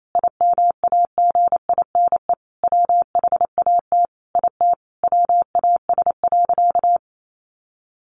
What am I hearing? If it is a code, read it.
Morse code, heard as a keyed tone, about 28 wpm, 700 Hz. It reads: IMAGINE WHAT IT WAS.
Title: Sentences from Top 1400 Words 28wpm